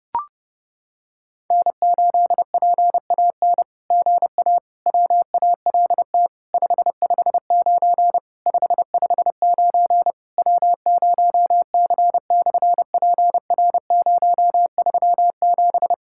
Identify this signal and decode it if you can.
Morse code, heard as a keyed tone, about 32 wpm, 700 Hz. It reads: N8PAN GA WALT 559 559 W0C/PR037
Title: SOTA QSO Practice 30wpm